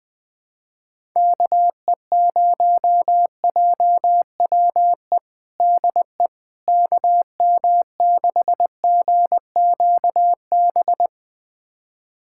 Morse code, heard as a keyed tone, about 20 wpm, 700 Hz. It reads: KE0JWE DE KM6GQB
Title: USA Callsign DE Callsign 20wpm